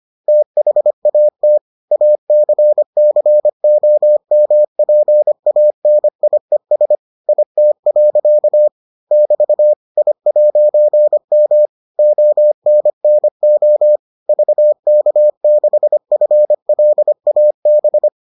Morse code, heard as a keyed tone, about 25 wpm, 600 Hz. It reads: THAT ACCOMPANIES IT. <BT> I'M ONNO VK6FLAB